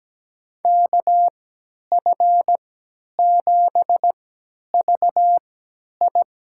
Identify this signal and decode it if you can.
Morse code, heard as a keyed tone, about 17 wpm, spaced as for 10 wpm, 700 Hz. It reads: KF7VI
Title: USA Call Signs - 3x Character Spacing - 17wpm